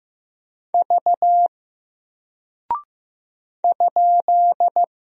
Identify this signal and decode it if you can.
Morse code, heard as a keyed tone, about 15 wpm, 700 Hz. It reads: V ?
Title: Single Character (Letter:Number:Punctuation:Prosign) Alt 15wpm